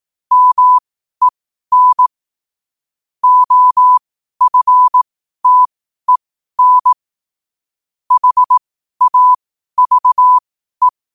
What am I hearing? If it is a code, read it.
Morse code, heard as a keyed tone, about 18 wpm, spaced as for 12 wpm, 1000 Hz. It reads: MEN OFTEN HAVE
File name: qotdfiles / 12wpm